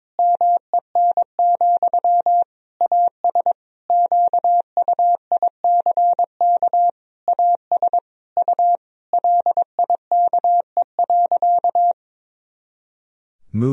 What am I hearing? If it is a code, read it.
Morse code, heard as a keyed tone, about 22 wpm, 700 Hz. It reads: MEN, AS QUICK AS U LIKE.
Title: Sentences from Top 500 Words and Abbreviations 22wpm